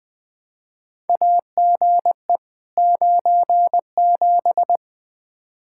Morse code, heard as a keyed tone, about 20 wpm, 700 Hz. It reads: AGE 97
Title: Practice QSO - Exchange 3 - More Info - 20wpm